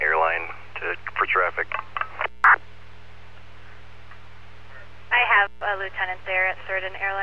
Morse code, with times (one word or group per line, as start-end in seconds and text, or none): none